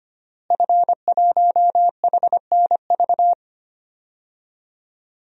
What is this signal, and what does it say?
Morse code, heard as a keyed tone, about 25 wpm, 700 Hz. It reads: F1HNV
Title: DXCC-100 Call Signs - 5 Characters - Repeated 2x - 25wpm